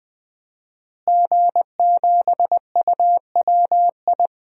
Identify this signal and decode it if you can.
Morse code, heard as a keyed tone, about 20 wpm, 700 Hz. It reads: G7UWI
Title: DXCC-100 Call Signs - 5 Characters - Repeated 3x - 20wpm